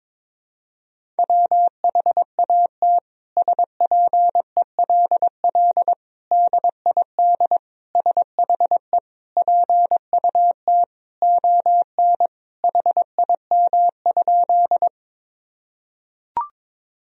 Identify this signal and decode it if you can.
Morse code, heard as a keyed tone, about 22 wpm, 700 Hz. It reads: WHAT SPELL DID SHE PUT ON HIM?
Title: Sentences from Top 1000 Words - Review Part 1 - 22wpm